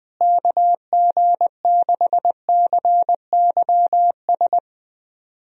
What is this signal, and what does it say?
Morse code, heard as a keyed tone, about 20 wpm, 700 Hz. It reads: KG6CYS
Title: USA Call Signs - 6 Characters - Repeated 2x - 20wpm